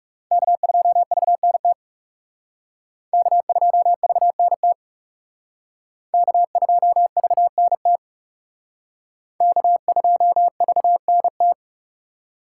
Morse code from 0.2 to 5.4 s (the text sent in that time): K2VNT K2VNT